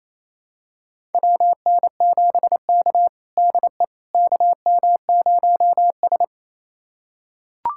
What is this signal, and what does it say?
Morse code, heard as a keyed tone, about 28 wpm, 700 Hz. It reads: WN7K DE KM0S
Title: USA Callsign DE Callsign - 4 Characters - 28wpm